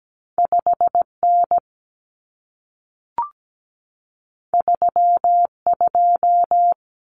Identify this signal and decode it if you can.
Morse code, heard as a keyed tone, about 17 wpm, 700 Hz. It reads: 5N 32
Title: Common Cut Numbers - 2-Digit 17wpm